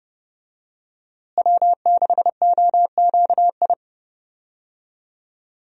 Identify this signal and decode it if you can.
Morse code, heard as a keyed tone, about 30 wpm, 700 Hz. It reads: W6OQI
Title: USA Call Signs - 5 Characters - Repeated 3x - 30wpm